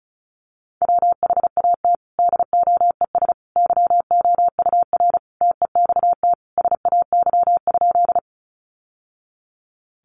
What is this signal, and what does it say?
Morse code, heard as a keyed tone, about 34 wpm, 700 Hz. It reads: WHAT DOES YOUR TEXT SAY?
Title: Sentences from Top 1200 Words 35wpm